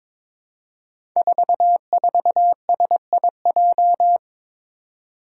Morse code, heard as a keyed tone, about 22 wpm, 700 Hz. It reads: 44SIJ
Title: Five Challenging Letter-Number Combinations 22wpm